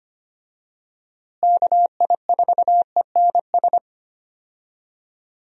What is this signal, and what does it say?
Morse code, heard as a keyed tone, about 25 wpm, 700 Hz. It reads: KI4ENS
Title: USA Call Signs - 6 Characters - Repeated 3x - 25wpm